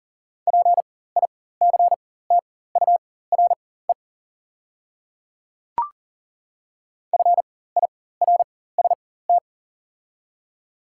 Morse code, heard as a keyed tone, about 40 wpm, spaced as for 19 wpm, 700 Hz. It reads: PICTURE FIRST
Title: 4x Character Spacing - Top 200 Words - 40wpm